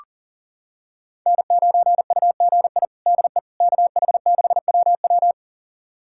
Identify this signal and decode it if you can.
Morse code, heard as a keyed tone, about 40 wpm, 700 Hz. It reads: N9UGI DE KH6WW
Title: USA Callsign DE Callsign - 5 Characters - 40wpm